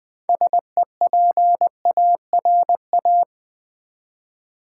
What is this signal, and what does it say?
Morse code, heard as a keyed tone, about 20 wpm, 700 Hz. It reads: SEPARA
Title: Build A Word - 3 Letters At A Time - 8-Letter Words - 20wpm